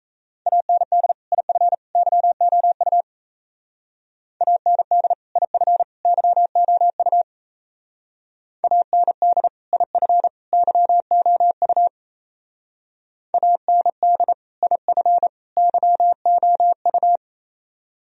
Morse code, heard as a keyed tone, about 34 wpm, 700 Hz. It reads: AND IF YOU AND IF YOU AND IF YOU AND IF YOU
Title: Speed-Racing - Sets of 3 Words 28wpm